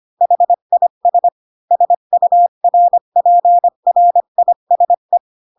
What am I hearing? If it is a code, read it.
Morse code, heard as a keyed tone, about 25 wpm, 700 Hz. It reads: HIS SURPRISE
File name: Pinocchio / Pinocchio ch3 20 25 ik1hge